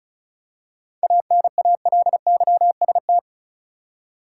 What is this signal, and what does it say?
Morse code, heard as a keyed tone, about 34 wpm, 700 Hz. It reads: ANALYST
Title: Seven Letter Words - 35wpm